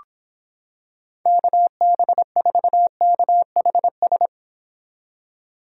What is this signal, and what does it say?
Morse code, heard as a keyed tone, about 26 wpm, 700 Hz. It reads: KB4KHS